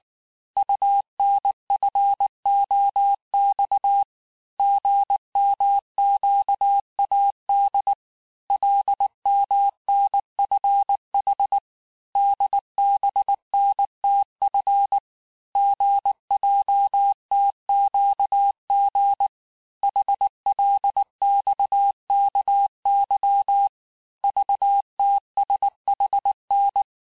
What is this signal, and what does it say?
Morse code, heard as a keyed tone, about 19 wpm, 800 Hz. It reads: UNFOX GMQAD LMNFH DBNTF GJTQG HLXKY VTSHN
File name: cours-du-lundi files / cours-1589-texte-chiffre-vitesse-18mpm